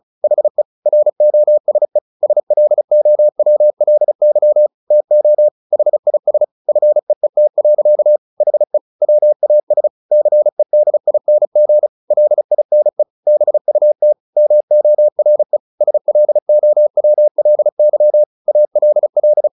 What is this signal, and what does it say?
Morse code, heard as a keyed tone, about 34 wpm, 600 Hz. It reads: HE ROSE SLOWLY TO HIS FEET. HE WAS CEDING LINE BUT MORE SLOWLY ALL
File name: the old man and the sea 2 35wpm